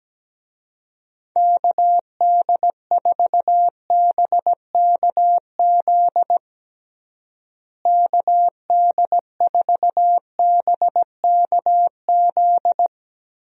Morse code, heard as a keyed tone, about 17 wpm, 700 Hz. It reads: KD4BKZ KD4BKZ
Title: USA Call Signs - 6 Characters - Repeated 3x - 17wpm